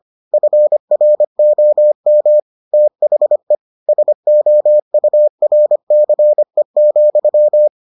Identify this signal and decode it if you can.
Morse code, heard as a keyed tone, about 25 wpm, 600 Hz. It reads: FROM THE SOURCE,